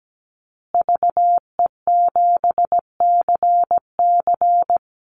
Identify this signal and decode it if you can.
Morse code, heard as a keyed tone, about 17 wpm, 700 Hz. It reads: VE7CC
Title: Call Signs Repeated 2x - 17wpm